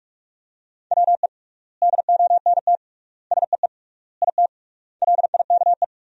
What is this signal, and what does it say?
Morse code, heard as a keyed tone, about 42 wpm, 700 Hz. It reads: WE DONT SEE IT LIKE
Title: 3x Word-Spacing - Sentences from Top 300 Words - 45wpm